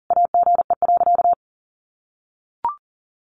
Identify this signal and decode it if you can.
Morse code, heard as a keyed tone, about 37 wpm, 700 Hz. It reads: AGE.
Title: Sentences from Top 1300 Words 40wpm